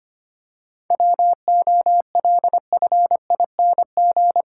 Morse code, heard as a keyed tone, about 25 wpm, 700 Hz. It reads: WOLFING
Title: Pangram Challenge v2 - 25wpm